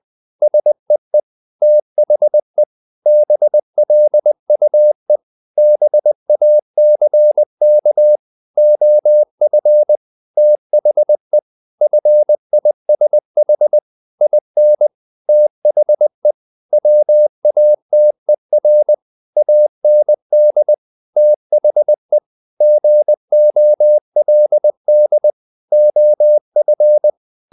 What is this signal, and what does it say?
Morse code, heard as a keyed tone, about 20 wpm, 600 Hz. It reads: SEE THE BLUE BACK OF THE FISH IN THE WATER AND THE GOLD OF
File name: the old man and the sea / the old man and the sea part 1 20wpm